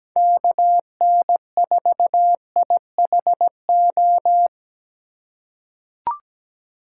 Morse code, heard as a keyed tone, about 17 wpm, 700 Hz. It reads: KN4IHO
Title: USA Call Signs - 6 Characters - 2x Character Spacing - 17wpm